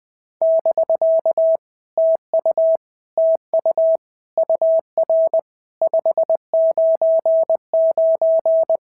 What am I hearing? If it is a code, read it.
Morse code, heard as a keyed tone, about 20 wpm, 650 Hz. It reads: <BK> TU TU UR 599
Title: POTA QSO Practice - 20wpm